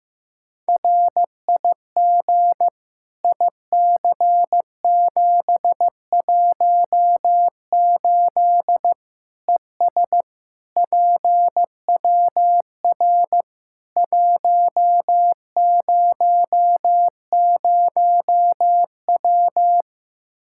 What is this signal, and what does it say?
Morse code, heard as a keyed tone, about 16 wpm, 700 Hz. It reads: RIG IC718 ES PWR 100W
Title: Practice QSO - Exchange 2 - Info - 15wpm